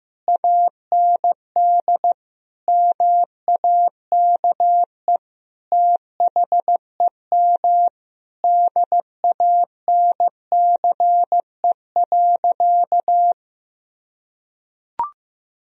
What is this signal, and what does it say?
Morse code, heard as a keyed tone, about 15 wpm, 700 Hz. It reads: AND MAKE THEM DANCE.
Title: Sentences from Top 1000 Words - Review Part 4 - 15wpm